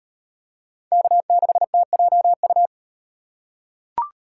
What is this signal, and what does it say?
Morse code, heard as a keyed tone, about 36 wpm, 700 Hz. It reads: K6TJU